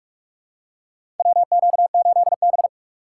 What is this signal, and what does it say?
Morse code, heard as a keyed tone, about 48 wpm, 700 Hz. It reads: WQ8B
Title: USA Call Signs - 4 Characters - 6x Character Spacing - 45wpm